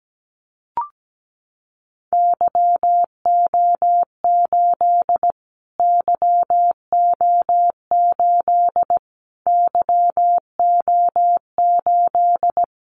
Morse code, heard as a keyed tone, about 17 wpm, 700 Hz. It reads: YO8 YO8 YO8
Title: Callsign Prefixes for Top 25 DXCC - Repeated 3x - 17wpm